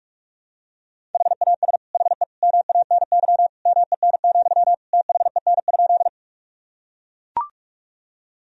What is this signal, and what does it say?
Morse code, heard as a keyed tone, about 42 wpm, 700 Hz. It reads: HAS HE MANY MEN, THEN?
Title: Sentences from Top 1000 Words - Review Part 1 - 45wpm